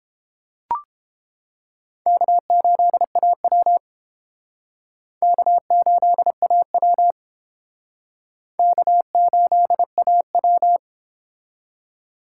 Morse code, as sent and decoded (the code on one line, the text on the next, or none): -.- ---.. .- .-- / -.- ---.. .- .-- / -.- ---.. .- .--
K8AW K8AW K8AW